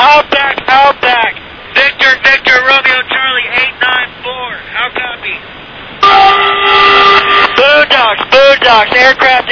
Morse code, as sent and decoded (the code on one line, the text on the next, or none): none